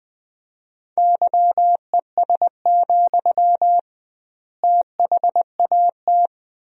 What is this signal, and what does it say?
Morse code, heard as a keyed tone, about 21 wpm, 700 Hz. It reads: YES, THAT
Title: 2x Word-Spacing - Sentences from Top 500 Words - 20wpm